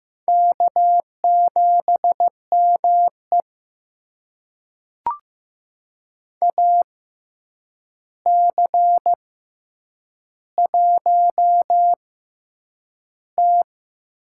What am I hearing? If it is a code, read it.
Morse code, heard as a keyed tone, about 15 wpm, spaced as for 8 wpm, 700 Hz. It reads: K7ME AC1T
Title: USA Call Signs - 4 Characters - 6x Character Spacing - 15wpm